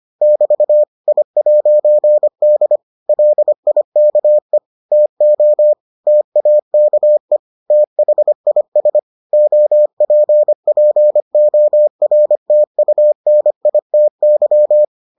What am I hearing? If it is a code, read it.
Morse code, heard as a keyed tone, about 25 wpm, 600 Hz. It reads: <BT> I'D LIKE TO TAKE THIS OPPORTUNITY